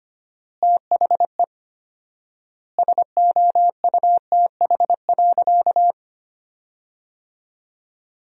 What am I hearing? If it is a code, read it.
Morse code, heard as a keyed tone, about 25 wpm, 700 Hz. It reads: THE SOUTH.
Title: 4x Word-Spacing - Sentences from Top 400 Words - 25wpm